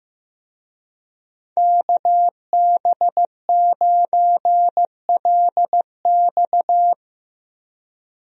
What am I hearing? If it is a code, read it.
Morse code, heard as a keyed tone, about 16 wpm, 700 Hz. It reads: KB9LX
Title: USA Call Signs - 5 Characters - Repeated 2x - 15wpm